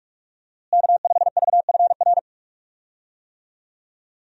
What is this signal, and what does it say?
Morse code, heard as a keyed tone, about 43 wpm, 700 Hz. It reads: K5VFR
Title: USA Call Signs - 5 Characters - Repeated 2x - 45wpm